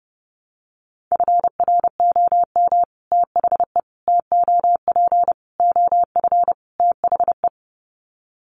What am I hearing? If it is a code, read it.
Morse code, heard as a keyed tone, about 30 wpm, 700 Hz. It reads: FROM THE TOP OF THE
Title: Sets of 5 Words Encore 30wpm